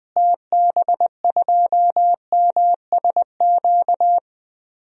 Morse code, heard as a keyed tone, about 20 wpm, 700 Hz. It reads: TB2MSQ
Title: DXCC-100 Call Signs - 6 Characters - Repeated 3x - 20wpm